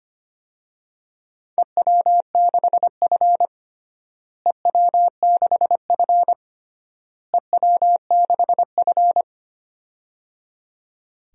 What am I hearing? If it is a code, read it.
Morse code, heard as a keyed tone, about 25 wpm, 700 Hz. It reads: EW6F EW6F EW6F
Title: DXCC-100 Call Signs - 4 Characters - Repeated 3x - 25wpm